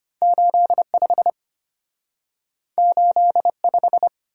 Morse code, heard as a keyed tone, about 34 wpm, 700 Hz. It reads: 85 85